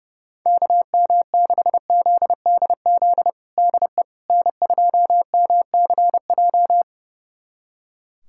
Morse code, heard as a keyed tone, about 32 wpm, 700 Hz. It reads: KM6ZDZ DE N2MCJ
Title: USA Callsign DE Callsign 30wpm